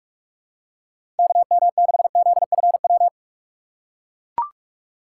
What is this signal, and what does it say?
Morse code, heard as a keyed tone, about 42 wpm, 700 Hz. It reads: KM6ZFW
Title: USA Call Signs - 6 Characters - 4x Character Spacing - 45wpm